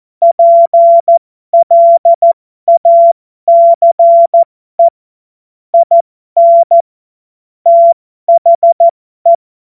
Morse code, heard as a keyed tone, about 14 wpm, spaced as for 12 wpm, 680 Hz. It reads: PLACE IN THE
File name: Morsekurs / Witze